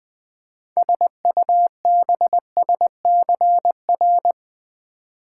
Morse code, heard as a keyed tone, about 20 wpm, 700 Hz. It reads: SUBSCR
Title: Build A Word - 6 Letters At A Time - 12-Letter Words - 20wpm